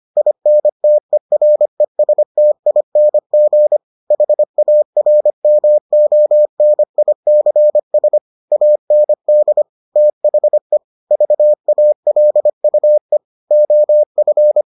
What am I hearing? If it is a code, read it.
Morse code, heard as a keyed tone, about 25 wpm, 600 Hz. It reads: INTERESTING HARMONICS AND THE VALUE OF